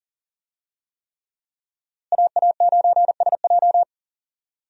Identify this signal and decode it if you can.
Morse code, heard as a keyed tone, about 40 wpm, 700 Hz. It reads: AA9SJ
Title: USA Call Signs - 5 Characters - Repeated 3x - 40wpm